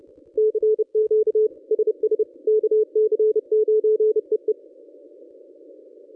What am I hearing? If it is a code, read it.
Morse code, heard as a keyed tone, about 31 wpm, 440 Hz. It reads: CQ SS KC9EE
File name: contests / ss-cw-k5tr-op-n5rz -1100